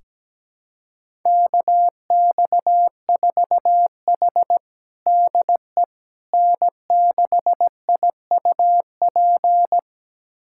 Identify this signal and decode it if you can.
Morse code, heard as a keyed tone, about 17 wpm, 700 Hz. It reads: KX4H DE N6IUP
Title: USA Callsign DE Callsign 17wpm